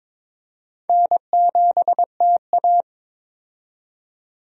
Morse code, heard as a keyed tone, about 22 wpm, 700 Hz. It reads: N7TA
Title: USA Call Signs - 4 Characters - 22wpm